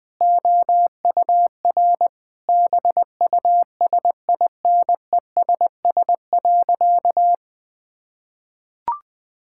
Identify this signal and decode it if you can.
Morse code, heard as a keyed tone, about 20 wpm, 700 Hz. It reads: OUR BUSINESS.
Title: Sentences from Top 1000 Words - Review Part 6 - 20wpm